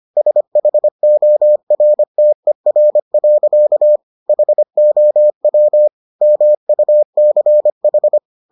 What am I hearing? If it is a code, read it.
Morse code, heard as a keyed tone, about 25 wpm, 600 Hz. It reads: SHORTER. HOW MUCH